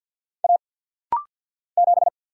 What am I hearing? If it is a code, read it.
Morse code, heard as a keyed tone, about 50 wpm, 700 Hz. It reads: A 7
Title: Single Letter-Number - Rapid-Fire 50wpm